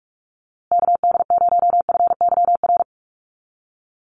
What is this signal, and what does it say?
Morse code, heard as a keyed tone, about 44 wpm, 700 Hz. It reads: KD0FYR
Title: USA Call Signs Repeated 2x 45wpm